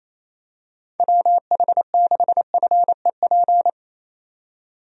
Morse code, heard as a keyed tone, about 27 wpm, 700 Hz. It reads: WH6FEP
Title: USA Call Signs - 6 Characters - 28wpm